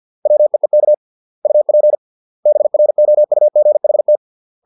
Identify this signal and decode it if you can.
Morse code, heard as a keyed tone, about 50 wpm, 600 Hz. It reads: WEEK UP BROUGHT